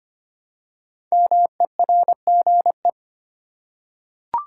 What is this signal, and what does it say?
Morse code, heard as a keyed tone, about 25 wpm, 700 Hz. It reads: MERGE
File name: 5 Letter Words - First 3 Letters Given - 25wpm